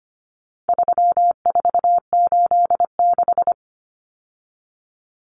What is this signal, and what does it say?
Morse code, heard as a keyed tone, about 25 wpm, 700 Hz. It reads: 3486
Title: Common Cut Numbers - 4-Digit 25wpm